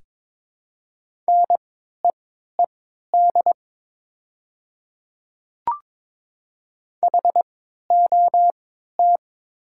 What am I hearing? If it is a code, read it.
Morse code, heard as a keyed tone, about 22 wpm, spaced as for 12 wpm, 700 Hz. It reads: NEED HOT
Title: 3x Character Spacing - Top 200 Words - 22wpm